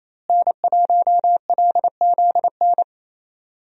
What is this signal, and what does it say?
Morse code, heard as a keyed tone, about 27 wpm, 700 Hz. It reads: N1LZN
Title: USA Call Signs - 5 Characters - Repeated 3x - 28wpm